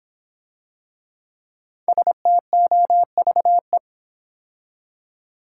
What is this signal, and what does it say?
Morse code, heard as a keyed tone, about 26 wpm, 700 Hz. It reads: STOVE